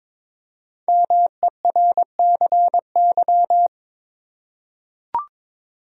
Morse code, heard as a keyed tone, about 22 wpm, 690 Hz. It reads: MERCY
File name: 5 Letter Words - First 3 Letters Given - 22wpm